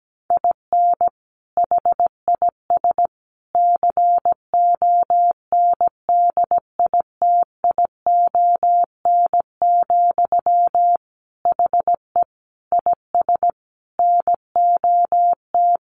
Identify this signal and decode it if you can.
Morse code, heard as a keyed tone, about 17 wpm, 700 Hz. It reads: IN HIS CONDITION, HE IS NOT
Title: Sentences from Top 1200 Words 17wpm